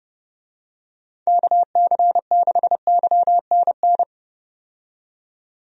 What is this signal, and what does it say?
Morse code, heard as a keyed tone, about 30 wpm, 700 Hz. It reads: KC6YNN
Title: USA Call Signs - 6 Characters - 30wpm